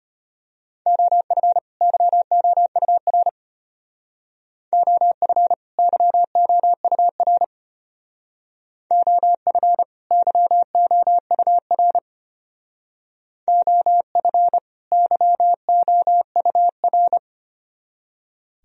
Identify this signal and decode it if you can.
Morse code, heard as a keyed tone, about 33 wpm, 700 Hz. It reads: OF YOUR OF YOUR OF YOUR OF YOUR